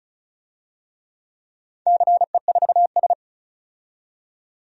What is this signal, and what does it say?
Morse code, heard as a keyed tone, about 36 wpm, 700 Hz. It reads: CE4S
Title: DXCC-100 Call Signs - 4 Characters - 35wpm